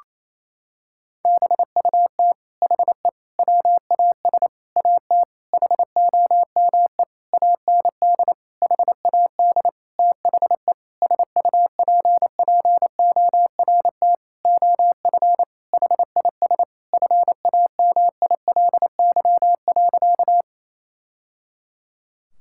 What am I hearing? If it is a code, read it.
Morse code, heard as a keyed tone, about 28 wpm, 700 Hz. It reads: BUT HE WAS AT HOME AND HAD THE SUPPORT OF HIS FAMILY.
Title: Sentences from Top 1000 Words - Review Part 5 - 28wpm